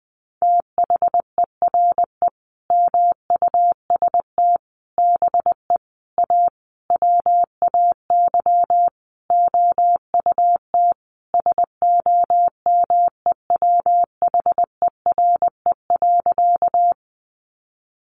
Morse code, heard as a keyed tone, about 19 wpm, 700 Hz. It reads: THERE MUST BE A WAY OUT SOMEWHERE.
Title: Sentences from Top 1400 Words 20wpm